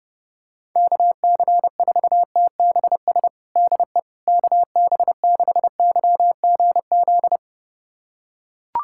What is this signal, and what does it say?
Morse code, heard as a keyed tone, about 30 wpm, 700 Hz. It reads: KC4TBS DE KB6YGZ
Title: USA Callsign DE Callsign - 6 Characters - 30wpm